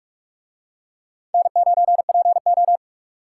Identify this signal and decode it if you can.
Morse code, heard as a keyed tone, about 44 wpm, 700 Hz. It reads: N9PO